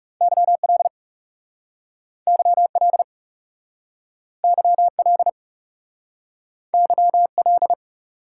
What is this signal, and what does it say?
Morse code, heard as a keyed tone, about 40 wpm, 700 Hz. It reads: YL YL YL YL